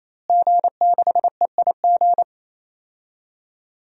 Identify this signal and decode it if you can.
Morse code, heard as a keyed tone, about 28 wpm, 700 Hz. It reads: G6EIG
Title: DXCC-100 Call Signs - 5 Characters - Repeated 3x - 28wpm